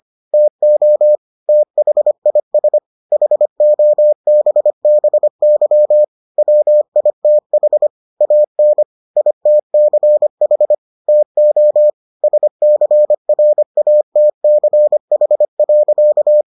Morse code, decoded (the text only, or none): TO THIS HOBBY WITH AN ITCH TO SCRATCH.